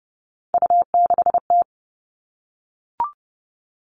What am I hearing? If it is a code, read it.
Morse code, heard as a keyed tone, about 30 wpm, 700 Hz. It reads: U6T
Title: Three Challenging Letter-Number Combinations 30wpm